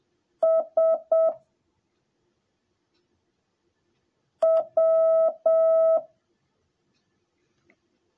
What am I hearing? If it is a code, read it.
Morse code, heard as a keyed tone, about 7 wpm, 650 Hz. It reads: S W
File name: jh8cbh / 2021cw / cw10a